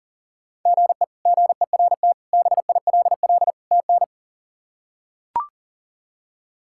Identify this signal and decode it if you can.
Morse code, heard as a keyed tone, about 40 wpm, 700 Hz. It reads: GE GERT BILL TN
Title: K1USN SST QSO Practice - 40wpm